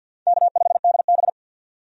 Morse code, heard as a keyed tone, about 47 wpm, 700 Hz. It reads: K5DB
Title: USA Call Signs - 4 Characters - 50wpm